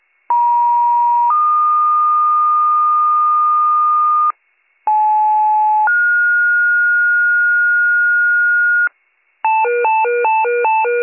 Two-tone page, heard as a one-second tone, then a three-second tone, about 950 then 1220 Hz, about 860 then 1430 Hz, with noise. It is heard from 0.3 to 8.9 s.